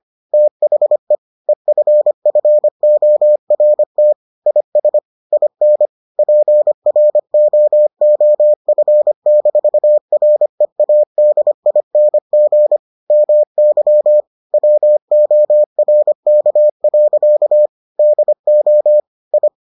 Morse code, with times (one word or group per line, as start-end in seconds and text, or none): 0.3-1.1 THE
1.5-4.1 EFFORT
4.5-5.0 IS
5.3-5.9 IN
6.2-12.8 PROOF-READING
13.1-14.2 MY
14.5-17.7 WORK.
18.0-19.0 DO
19.3-19.5 I